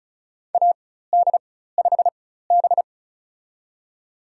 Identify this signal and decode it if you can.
Morse code, heard as a keyed tone, about 35 wpm, spaced as for 16 wpm, 700 Hz. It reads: AD5B
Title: USA Call Signs - 4 Characters - 4x Character Spacing - 35wpm